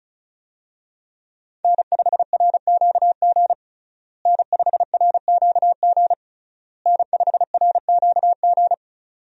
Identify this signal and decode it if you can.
Morse code, heard as a keyed tone, about 34 wpm, 700 Hz. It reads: N5RQG N5RQG N5RQG
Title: USA Call Signs - 5 Characters - Repeated 3x - 35wpm